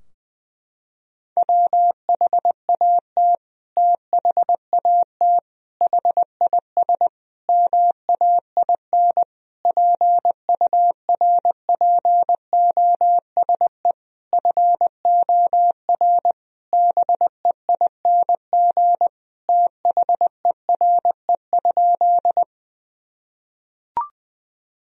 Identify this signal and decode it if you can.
Morse code, heard as a keyed tone, about 20 wpm, 700 Hz. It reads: WHAT THAT HIS MAIN PURPOSE FOR BEING THERE?
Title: Sentences from Top 1000 Words - Review Part 2 - 20wpm